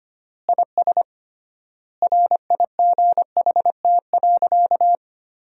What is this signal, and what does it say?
Morse code, heard as a keyed tone, about 25 wpm, 700 Hz. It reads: IS RIGHT.
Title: 3x Word-Spacing - Sentences from Top 300 Words - 25wpm